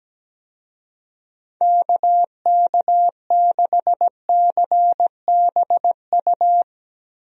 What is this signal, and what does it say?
Morse code, heard as a keyed tone, about 17 wpm, 700 Hz. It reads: KK6CBU
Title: USA Call Signs - 6 Characters - Repeated 3x - 17wpm